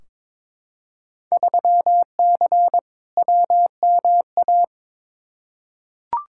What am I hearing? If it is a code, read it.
Morse code, heard as a keyed tone, about 22 wpm, 700 Hz. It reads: 3C WMA
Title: Field Day Exchanges Encore - 22wpm